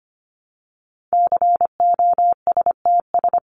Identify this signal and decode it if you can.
Morse code, heard as a keyed tone, about 25 wpm, 700 Hz. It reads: COSTS
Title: ES:S Suffix - Top 100 Words with Base Words 25wpm